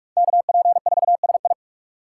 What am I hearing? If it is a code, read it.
Morse code, heard as a keyed tone, about 46 wpm, 700 Hz. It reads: KP4SI
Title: USA Call Signs - 3x Character Spacing - 45wpm